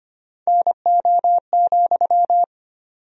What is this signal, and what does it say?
Morse code, heard as a keyed tone, about 25 wpm, 700 Hz. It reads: NO,
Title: 4x Word-Spacing - Sentences from Top 300 Words - 25wpm